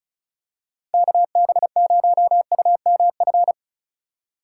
Morse code, heard as a keyed tone, about 34 wpm, 700 Hz. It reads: KB0UMF
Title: USA Call Signs - 6 Characters - 4x Character Spacing - 35wpm